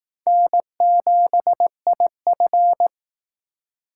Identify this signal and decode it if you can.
Morse code, heard as a keyed tone, about 18 wpm, 700 Hz. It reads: N7IF
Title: Speed Racing - USA 1x2 Call Signs 15wpm